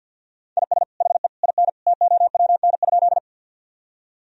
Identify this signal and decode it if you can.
Morse code, heard as a keyed tone, about 46 wpm, 700 Hz. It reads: IS HE IN TOWN?
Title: Sentences from Top 1000 Words - Review Part 2 - 50wpm